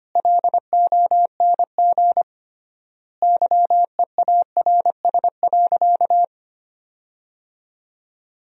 Morse code, heard as a keyed tone, about 25 wpm, 700 Hz. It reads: LONG YEARS.
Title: 3x Word-Spacing - Sentences from Top 200 Words - 25wpm